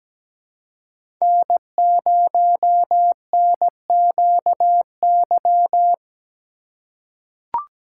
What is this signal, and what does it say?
Morse code, heard as a keyed tone, about 17 wpm, 700 Hz. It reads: N0NQY
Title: USA Call Signs - 2x Character Spacing - 17wpm